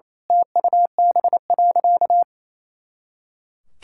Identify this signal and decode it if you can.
Morse code, heard as a keyed tone, about 28 wpm, 700 Hz. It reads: TUB.
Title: Pangram Challenge v2 - 28wpm